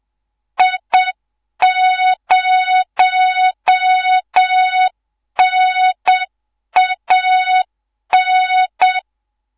Morse code, heard as a keyed tone, about 7 wpm, 3700 Hz, with noise. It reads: I0NAN
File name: Morze-HA2MN / hangp / 2 03 GYAKORLAT